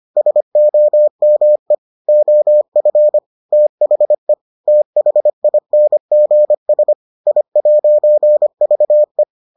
Morse code, heard as a keyed tone, about 25 wpm, 600 Hz. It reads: SOME OF THE THINGS I'VE